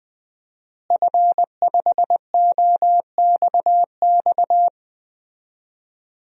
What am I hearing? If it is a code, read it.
Morse code, heard as a keyed tone, about 20 wpm, 700 Hz. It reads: F5OXX
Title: DXCC-100 Call Signs - 5 Characters - Repeated 2x - 20wpm